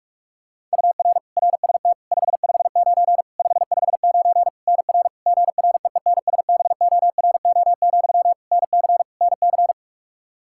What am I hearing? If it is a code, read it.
Morse code, heard as a keyed tone, about 46 wpm, 700 Hz. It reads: UR RST 559 559 NR GREENSBORO, NC NC
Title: K1USN SST QSO Practice 45wpm